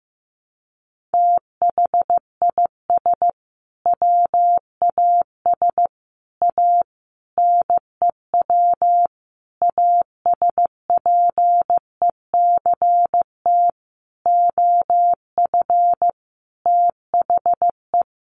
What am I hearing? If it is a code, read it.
Morse code, heard as a keyed tone, about 14 wpm, 700 Hz. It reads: THIS WAS A NEW ASPECT OF THE
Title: Sentences from Top 1300 Words 15wpm